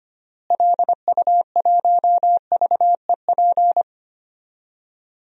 Morse code, heard as a keyed tone, about 25 wpm, 700 Hz. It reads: LU1VEP
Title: DXCC-100 Call Signs - 6 Characters - Repeated 2x - 25wpm